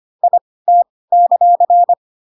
Morse code, heard as a keyed tone, about 25 wpm, spaced as for 18 wpm, 700 Hz. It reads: IT;
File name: Pinocchio / Pinocchio ch7 12 25 ik1hge